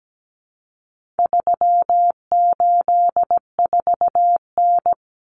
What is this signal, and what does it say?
Morse code, heard as a keyed tone, about 17 wpm, 700 Hz. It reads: 384N
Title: Common Cut Numbers - 4-Digit 17wpm